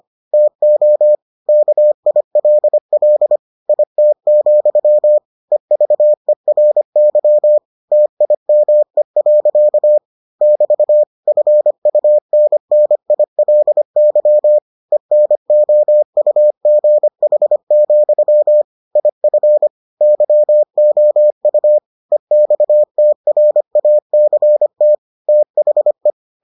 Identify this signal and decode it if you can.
Morse code, heard as a keyed tone, about 25 wpm, 600 Hz. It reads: TO KILL IT, EVERY TIME. <BT> FUNNILY ENOUGH, IF YOU EXTRACT THE